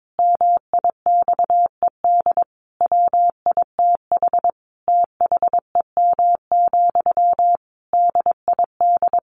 Morse code, heard as a keyed tone, about 22 wpm, 700 Hz. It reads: MIXED WITH THEM, DID
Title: Sentences from Top 1500 Words 22wpm